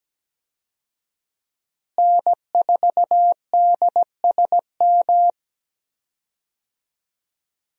Morse code, heard as a keyed tone, about 17 wpm, 700 Hz. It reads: N4DSM
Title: USA Call Signs - 5 Characters - Repeated 3x - 17wpm